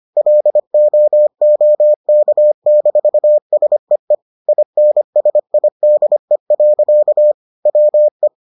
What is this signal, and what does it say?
Morse code, heard as a keyed tone, about 25 wpm, 600 Hz. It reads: LOOK-SEE INSIDE. WE